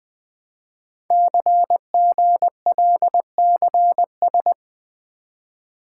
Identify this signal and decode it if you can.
Morse code, heard as a keyed tone, about 20 wpm, 700 Hz. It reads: CGLCS
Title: Code Groups - 5 Letters - 20wpm